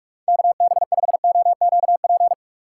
Morse code, heard as a keyed tone, about 47 wpm, 700 Hz. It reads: KB5OQP
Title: USA Call Signs - 3x Character Spacing - 45wpm